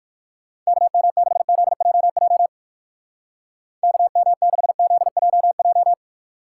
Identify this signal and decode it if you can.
Morse code, heard as a keyed tone, about 54 wpm, 700 Hz. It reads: KM6ZJJ KM6ZJJ
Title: Speed Racing - USA 2x3 Call Signs 45wpm